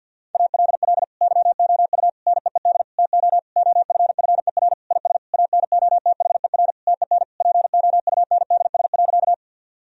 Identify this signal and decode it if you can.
Morse code, heard as a keyed tone, about 50 wpm, 700 Hz. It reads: ALL YOU NEED TO OFFER IS ANOTHER TEN POUNDS.
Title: Sentences from Top 1000 Words - Review Part 4 - 50wpm